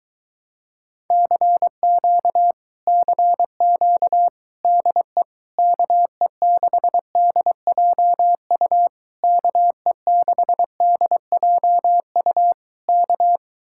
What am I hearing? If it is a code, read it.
Morse code, heard as a keyed tone, about 23 wpm, 700 Hz. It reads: CQ CQ DE KE6DJU KE6DJU K